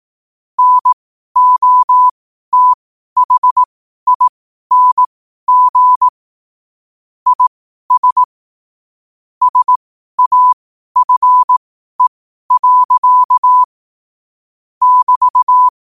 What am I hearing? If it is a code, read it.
Morse code, heard as a keyed tone, about 18 wpm, spaced as for 12 wpm, 1000 Hz. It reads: NOTHING IS SAFE. <BT>
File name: qotdfiles / 12wpm